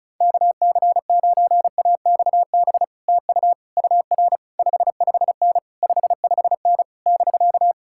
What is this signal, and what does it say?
Morse code, heard as a keyed tone, about 40 wpm, 700 Hz. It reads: KC9AXB TU UR 55N 55N <BK>